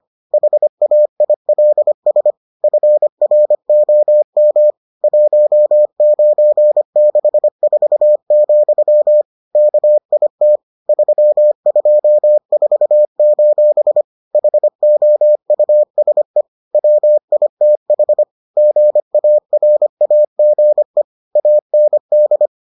Morse code, heard as a keyed tone, about 25 wpm, 600 Hz. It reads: HAILS FROM 1964, KIT 324: HOUSE WITH GARAGE AND